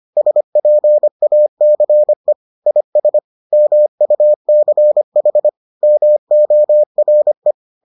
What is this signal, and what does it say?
Morse code, heard as a keyed tone, about 25 wpm, 600 Hz. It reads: SPACE IS MUCH MORE